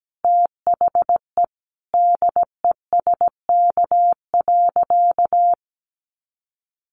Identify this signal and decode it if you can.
Morse code, heard as a keyed tone, about 17 wpm, 700 Hz. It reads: THE DESK.
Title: A Sound Of Thunder 17wpm